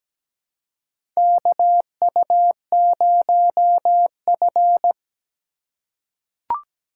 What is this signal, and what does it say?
Morse code, heard as a keyed tone, about 17 wpm, 700 Hz. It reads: KU0F
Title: USA Call Signs - 4 Characters - 2x Character Spacing - 17wpm